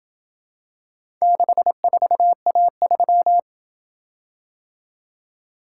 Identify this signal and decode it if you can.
Morse code, heard as a keyed tone, about 26 wpm, 700 Hz. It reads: 64A3